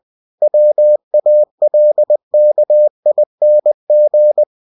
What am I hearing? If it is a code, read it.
Morse code, heard as a keyed tone, about 20 wpm, 600 Hz. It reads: WALKING
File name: cw-books / winnie-the-pooh / winnie-20wpm-page-0021